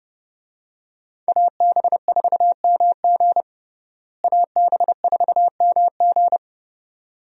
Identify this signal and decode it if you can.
Morse code, heard as a keyed tone, about 30 wpm, 700 Hz. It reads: AB4MG AB4MG
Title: USA Call Signs - 5 Characters - Repeated 2x - 30wpm